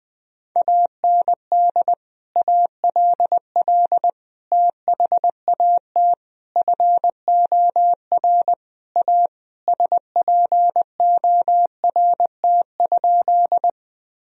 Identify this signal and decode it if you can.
Morse code, heard as a keyed tone, about 20 wpm, 700 Hz. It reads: AND ALL THAT FOR A SPORT?
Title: Sentences from Top 1000 Words - Review Part 4 - 20wpm